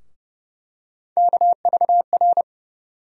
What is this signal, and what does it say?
Morse code, heard as a keyed tone, about 29 wpm, 700 Hz. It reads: KVR
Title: USA Call Sign Postfixes - 3 Letters 30wpm